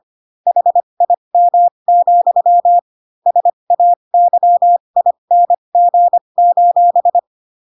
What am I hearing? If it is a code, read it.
Morse code, heard as a keyed tone, about 25 wpm, spaced as for 22 wpm, 700 Hz. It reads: HIM, SAYING:
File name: Pinocchio / Pinocchio ch7 18 25 ik1hge